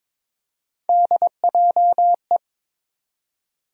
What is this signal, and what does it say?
Morse code, heard as a keyed tone, about 22 wpm, 700 Hz. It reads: DJE
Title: USA Call Sign Postfixes - 3 Letters 22wpm